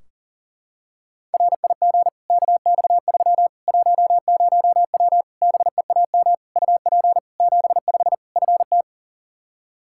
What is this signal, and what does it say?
Morse code, heard as a keyed tone, about 40 wpm, 700 Hz. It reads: RIG KX3 10W BEAM UP 75 FT
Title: Practice QSO - Exchange 2 - Info - 40wpm